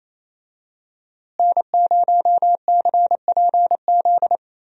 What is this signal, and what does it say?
Morse code, heard as a keyed tone, about 28 wpm, 700 Hz. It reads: N0CPZ
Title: USA Call Signs - 5 Characters - Repeated 3x - 28wpm